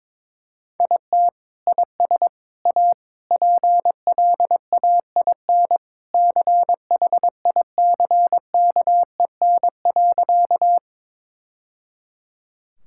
Morse code, heard as a keyed tone, about 22 wpm, 700 Hz. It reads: IT IS A PLAIN CHICKEN.
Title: Sentences from Top 1000 Words - Review Part 5 - 22wpm